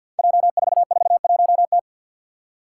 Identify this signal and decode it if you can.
Morse code, heard as a keyed tone, about 46 wpm, 700 Hz. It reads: J441T
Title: Five Challenging Letter-Number Combinations 50wpm